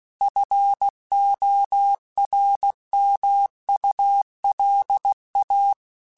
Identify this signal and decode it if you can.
Morse code, heard as a keyed tone, about 16 wpm, 800 Hz. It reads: FORMULA
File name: Technical Subjects / Technical Subjects-mp3s / XC @15